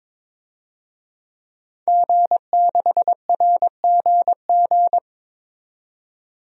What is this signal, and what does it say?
Morse code, heard as a keyed tone, about 22 wpm, 700 Hz. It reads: G6RGG
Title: DXCC-100 Call Signs - 5 Characters - 22wpm